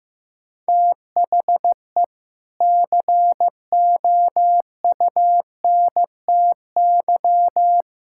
Morse code, heard as a keyed tone, about 15 wpm, 700 Hz. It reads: THE COUNTY
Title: Sentences from Top 1600 Words 15wpm